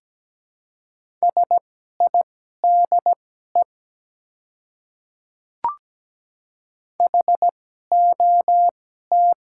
Morse code, heard as a keyed tone, about 17 wpm, spaced as for 12 wpm, 700 Hz. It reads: SIDE HOT
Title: 2x Character Spacing - Top 100 Words - 17wpm